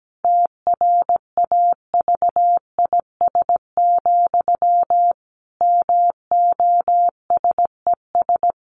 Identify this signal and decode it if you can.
Morse code, heard as a keyed tone, about 17 wpm, 700 Hz. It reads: TRAVIS, MOSES
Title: A Sound Of Thunder 17wpm